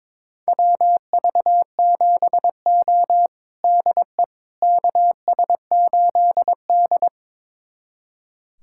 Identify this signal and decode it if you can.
Morse code, heard as a keyed tone, about 22 wpm, 700 Hz. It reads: WV7O DE KS8D
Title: USA Callsign DE Callsign - 4 Characters - 22wpm